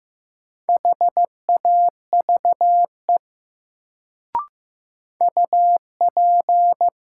Morse code, heard as a keyed tone, about 15 wpm, 700 Hz. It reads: HAVE UP
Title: Sets of 2 Words - New Word Drill - 15wpm